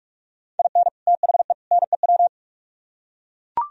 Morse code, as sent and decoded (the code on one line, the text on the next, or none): .. -. / - .... / -. .--
IN THE NEW